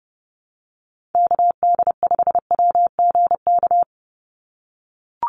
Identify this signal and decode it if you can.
Morse code, heard as a keyed tone, about 30 wpm, 700 Hz. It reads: KD5WGK
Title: USA Call Signs 30wpm